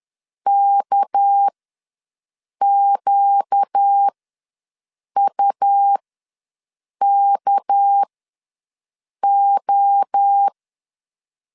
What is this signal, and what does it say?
Morse code, heard as a keyed tone, about 10 wpm, spaced as for 6 wpm, 800 Hz. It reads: KQUKO
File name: LEZIONI / 6 PKX0 / 6 PK ws 10wpm 3745